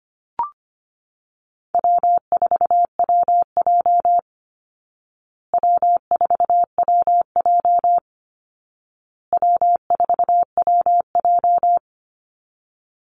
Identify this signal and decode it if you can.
Morse code, heard as a keyed tone, about 25 wpm, 700 Hz. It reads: W4WJ W4WJ W4WJ
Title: Easy Call Signs Repeated 3x - 25wpm